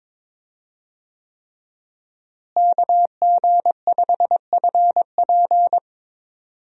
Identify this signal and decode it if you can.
Morse code, heard as a keyed tone, about 22 wpm, 700 Hz. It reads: KG5FP
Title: USA Call Signs - 5 Characters - Repeated 3x - 22wpm